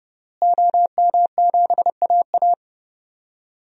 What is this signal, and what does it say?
Morse code, heard as a keyed tone, about 30 wpm, 700 Hz. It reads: OM7AA
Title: DXCC-100 Call Signs - 5 Characters - Repeated 3x - 30wpm